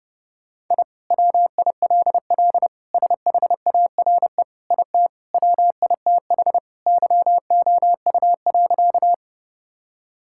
Morse code, heard as a keyed tone, about 32 wpm, 700 Hz. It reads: I WILL SHARE IT WITH YOU.
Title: Sentences from Top 1000 Words - Review Part 6 - 30wpm